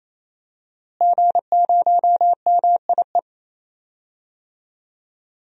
Morse code, heard as a keyed tone, about 28 wpm, 700 Hz. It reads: G0MIE
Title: DXCC-100 Call Signs - 5 Characters - Repeated 3x - 28wpm